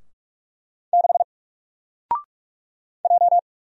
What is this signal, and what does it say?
Morse code, heard as a keyed tone, about 44 wpm, 700 Hz. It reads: 6 J
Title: Single Character (Letter:Number:Punctuation:Prosign) 45wpm